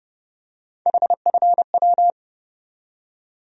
Morse code, heard as a keyed tone, about 32 wpm, 700 Hz. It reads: HFW
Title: USA Call Sign Postfixes - 3 Letters 30wpm